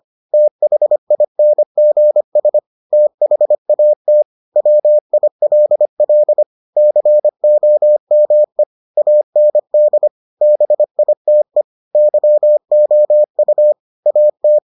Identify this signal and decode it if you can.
Morse code, heard as a keyed tone, about 25 wpm, 600 Hz. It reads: THINGS THAT WILL COME AND BITE YOU AT